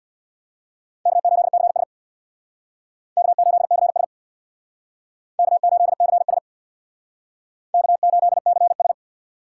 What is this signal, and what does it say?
Morse code, heard as a keyed tone, about 61 wpm, 700 Hz. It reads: K8CS K8CS K8CS K8CS
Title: Speed Racing - USA 1x2 Call Signs 50wpm